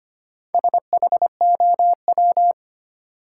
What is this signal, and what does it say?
Morse code, heard as a keyed tone, about 25 wpm, 700 Hz. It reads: SHOW
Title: Sentences from Top 200 Words - New Word Drill - 25wpm